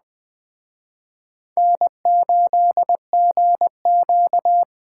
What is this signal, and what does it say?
Morse code, heard as a keyed tone, about 20 wpm, 700 Hz. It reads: N8GQ
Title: Speed Racing - USA 1x2 Call Signs 17wpm